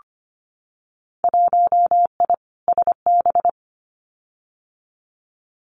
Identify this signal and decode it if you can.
Morse code, heard as a keyed tone, about 25 wpm, 700 Hz. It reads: 1I SB
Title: Winter Field Day Exchanges 25wpm